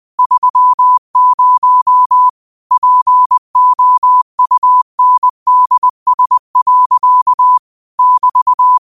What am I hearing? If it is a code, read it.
Morse code, heard as a keyed tone, about 20 wpm, 1000 Hz. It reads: 30 POUNDS. <BT>
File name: qotdfiles / 20wpm